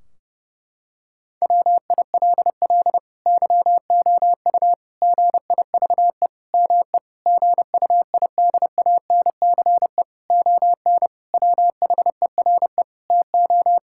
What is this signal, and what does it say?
Morse code, heard as a keyed tone, about 32 wpm, 700 Hz. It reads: WILL YOU GIVE ME GUIDANCE ON WHERE TO
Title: Sentences from Top 1600 Words 30wpm